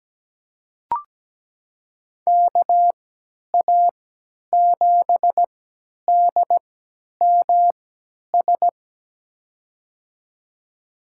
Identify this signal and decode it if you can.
Morse code, heard as a keyed tone, about 17 wpm, spaced as for 10 wpm, 700 Hz. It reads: KA7DMS
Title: USA Call Signs - 6 Characters - 3x Character Spacing - 17wpm